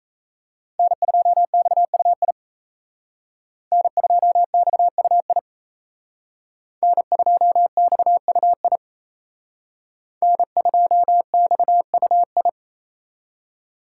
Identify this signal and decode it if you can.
Morse code, heard as a keyed tone, about 35 wpm, 700 Hz. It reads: N2XUI N2XUI N2XUI N2XUI